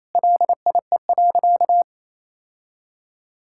Morse code, heard as a keyed tone, about 27 wpm, 700 Hz. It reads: LIE.
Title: Sentences from Top 1600 Words 28wpm